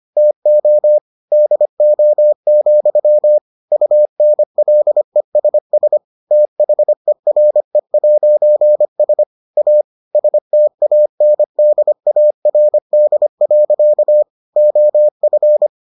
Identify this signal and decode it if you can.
Morse code, heard as a keyed tone, about 25 wpm, 600 Hz. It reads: TO DO, UNLESS THERE'S A STANDARD. OF